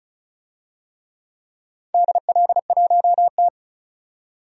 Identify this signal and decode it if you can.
Morse code, heard as a keyed tone, about 34 wpm, 700 Hz. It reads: DL1T
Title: DXCC-100 Call Signs - 4 Characters - 35wpm